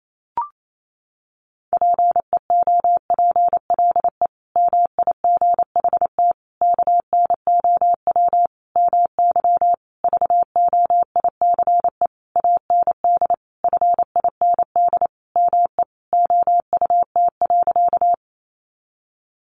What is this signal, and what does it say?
Morse code, heard as a keyed tone, about 28 wpm, 700 Hz. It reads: PEOPLE MIGHT KNOW MY VOICE AND FIND ME OUT.
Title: Sentences from Top 500 Words 28wpm